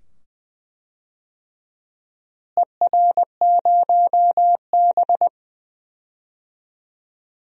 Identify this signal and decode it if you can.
Morse code, heard as a keyed tone, about 20 wpm, 700 Hz. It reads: ER0B
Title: DXCC-100 Call Signs - 4 Characters - Repeated 2x - 20wpm